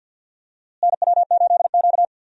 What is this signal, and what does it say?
Morse code, heard as a keyed tone, about 52 wpm, 700 Hz. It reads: NW8Q
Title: USA Call Signs - 4 Characters - Repeated 3x - 50wpm